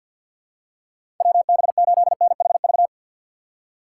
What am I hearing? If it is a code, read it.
Morse code, heard as a keyed tone, about 52 wpm, 690 Hz. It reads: WB8NHV
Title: USA Call Signs - 2x Character Spacing - 50wpm